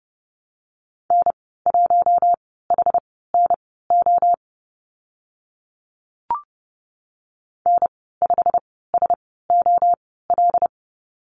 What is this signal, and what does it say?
Morse code, heard as a keyed tone, about 30 wpm, spaced as for 17 wpm, 700 Hz. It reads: N1HNO N5SOL
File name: Lesson 15g - Number 5 - Call Signs 30wpm